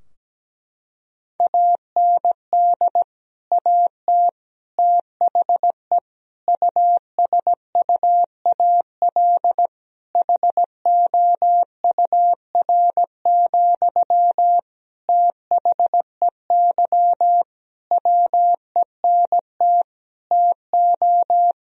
Morse code, heard as a keyed tone, about 17 wpm, 700 Hz. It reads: AND AT THE USUAL HOUR, THEY WENT TO
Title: Sentences from Top 1000 Words - Review Part 2 - 17wpm